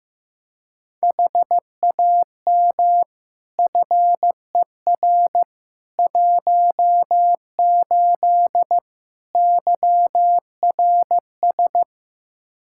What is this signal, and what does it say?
Morse code, heard as a keyed tone, about 15 wpm, 700 Hz. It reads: HAM FER 18 YRS
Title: Practice QSO - Exchange 3 - More Info - 15wpm